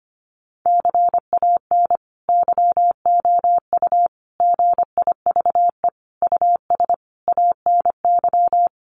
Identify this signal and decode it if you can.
Morse code, heard as a keyed tone, about 25 wpm, 700 Hz. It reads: CAN YOU GIVE US ANY
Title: Sentences from Top 1100 Words 25wpm